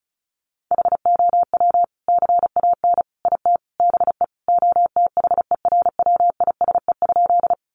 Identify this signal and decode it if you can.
Morse code, heard as a keyed tone, about 34 wpm, 700 Hz. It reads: HOW CAN IT BE OTHERWISE?
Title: Sentences from Top 1200 Words 35wpm